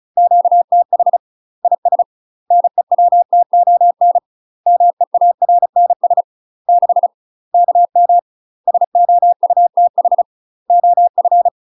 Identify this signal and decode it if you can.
Morse code, heard as a keyed tone, about 36 wpm, 700 Hz. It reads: QTH IS NEWTON MEARNS 6 KM SOUTH OF
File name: UK MASTER QSOs TRAINING / 35WPM / UK MASTER QSOs 0038